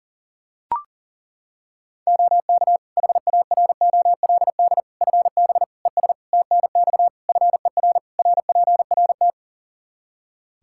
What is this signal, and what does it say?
Morse code, heard as a keyed tone, about 40 wpm, 700 Hz. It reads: OK HAROLD FB ES TNX FER RPRT
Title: Practice QSO - Exchange 2 - Info - 40wpm